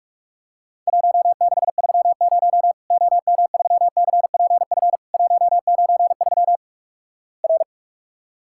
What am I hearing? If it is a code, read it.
Morse code, heard as a keyed tone, about 46 wpm, 700 Hz. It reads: 1630 OM3CPF 193 R
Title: Worked All Europe DX Contest - QTC Exchange - 45wpm